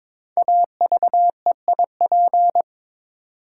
Morse code, heard as a keyed tone, about 22 wpm, 700 Hz. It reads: AVEIP
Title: Code Groups - 5 Letters - 22wpm